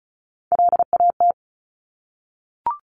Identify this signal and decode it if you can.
Morse code, heard as a keyed tone, about 36 wpm, 690 Hz. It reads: LAT
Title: Top Three Letter Combinations in Words 35wpm